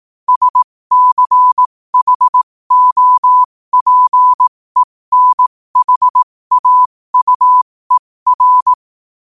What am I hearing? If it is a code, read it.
Morse code, heard as a keyed tone, about 18 wpm, spaced as for 15 wpm, 1000 Hz. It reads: SCHOPENHAUER
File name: qotdfiles / 15wpm